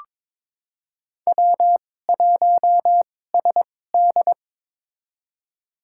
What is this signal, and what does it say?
Morse code, heard as a keyed tone, about 22 wpm, spaced as for 16 wpm, 700 Hz. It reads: W1SD
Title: USA Call Signs - 2x Character Spacing - 22wpm